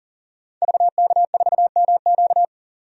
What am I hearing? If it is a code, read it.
Morse code, heard as a keyed tone, about 39 wpm, 700 Hz. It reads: VK4MQ
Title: DXCC-100 Call Signs - 5 Characters - 40wpm